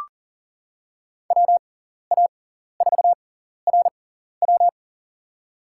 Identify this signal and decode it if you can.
Morse code, heard as a keyed tone, about 40 wpm, spaced as for 14 wpm, 700 Hz. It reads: WA4RW
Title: USA Call Signs - 5 Characters - 6x Character Spacing - 40wpm